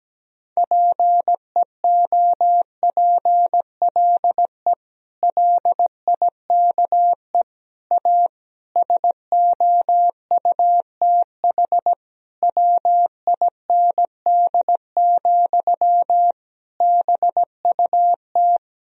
Morse code, heard as a keyed tone, about 17 wpm, 700 Hz. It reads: PEOPLE LIKE A SOUTH WIND, BUT
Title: Sentences from Top 1000 Words - Review Part 3 - 17wpm